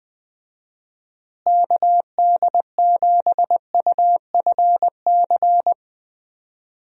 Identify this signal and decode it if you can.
Morse code, heard as a keyed tone, about 20 wpm, 700 Hz. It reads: KD7UFC
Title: USA Call Signs - 6 Characters - Repeated 3x - 20wpm